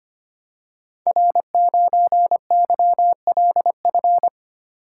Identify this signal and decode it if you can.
Morse code, heard as a keyed tone, about 25 wpm, 700 Hz. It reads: R9YLF
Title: Code Groups - 5 Letter:Numbers - 25wpm